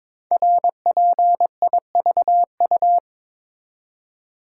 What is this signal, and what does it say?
Morse code, heard as a keyed tone, about 22 wpm, 700 Hz. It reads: RPIVU
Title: Code Groups - 5 Letters - 22wpm